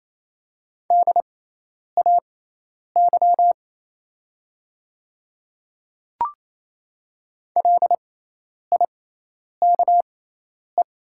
Morse code, heard as a keyed tone, about 28 wpm, spaced as for 10 wpm, 700 Hz. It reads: DAY LIKE
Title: 6x Character Spacing - Top 300 Words - 28wpm